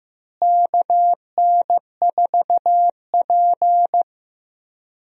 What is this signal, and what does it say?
Morse code, heard as a keyed tone, about 15 wpm, 700 Hz. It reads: KN4P
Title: USA Call Signs - 4 Characters - Repeated 3x - 15wpm